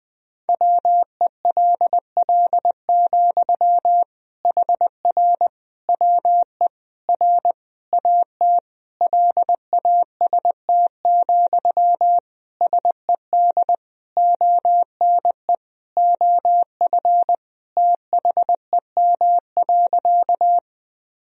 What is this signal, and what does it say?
Morse code, heard as a keyed tone, about 20 wpm, 700 Hz. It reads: WELL, HR WE R AT LAST, SED ONE OF THEM.
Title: Sentences from Top 300 Words and Abbreviations 20wpm